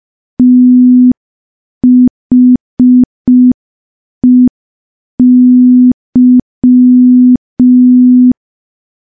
Morse code, heard as a keyed tone, about 5 wpm, 250 Hz. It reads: THEY
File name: morse / morse-training-05WPM